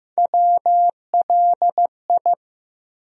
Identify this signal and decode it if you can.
Morse code, heard as a keyed tone, about 15 wpm, 700 Hz. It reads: WLI
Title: USA Call Sign Postfixes - 3 Letters 15wpm